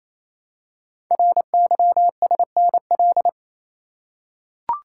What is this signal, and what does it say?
Morse code, heard as a keyed tone, about 29 wpm, 700 Hz. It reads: RYSNL
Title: Code Groups - 5 Letters - 28wpm